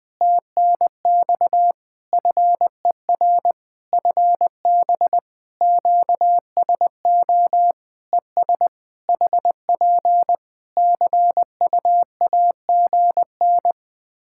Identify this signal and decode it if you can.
Morse code, heard as a keyed tone, about 20 wpm, 700 Hz. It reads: TNX FER FB QSO ES HP CUAGN
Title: Practice QSO - Exchange 4 - Ending - 20wpm